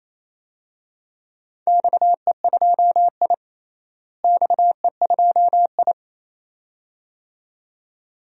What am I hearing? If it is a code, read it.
Morse code, heard as a keyed tone, about 28 wpm, 700 Hz. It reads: XE2I XE2I
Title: DXCC-100 Call Signs - 4 Characters - Repeated 2x - 28wpm